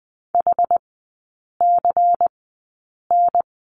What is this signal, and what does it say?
Morse code, heard as a keyed tone, about 20 wpm, spaced as for 8 wpm, 700 Hz. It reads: HCN
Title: 218 Lesson 35e - Number 0 - Three Characters 20wpm